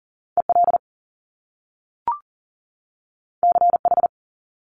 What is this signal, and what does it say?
Morse code, heard as a keyed tone, about 40 wpm, 700 Hz. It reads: EL CH
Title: Top Two Letter Combinations in Words 40wpm